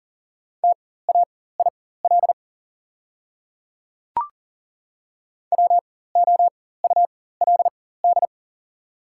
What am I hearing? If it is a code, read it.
Morse code, heard as a keyed tone, about 40 wpm, spaced as for 19 wpm, 700 Hz. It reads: TAIL WOULD
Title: 4x Character Spacing - Top 500 Words - 40wpm